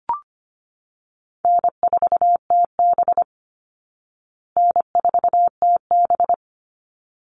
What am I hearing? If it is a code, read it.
Morse code, heard as a keyed tone, about 25 wpm, 700 Hz. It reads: N4TB N4TB
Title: Easy Call Signs Repeated 2x - 25wpm